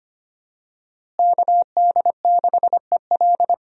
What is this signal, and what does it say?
Morse code, heard as a keyed tone, about 25 wpm, 700 Hz. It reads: KD6EL
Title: USA Call Signs - 5 Characters - Repeated 2x - 25wpm